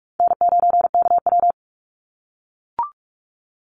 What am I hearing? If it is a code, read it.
Morse code, heard as a keyed tone, about 44 wpm, 700 Hz. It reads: N9KW
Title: NA SOTA Chaser Callsigns 45wpm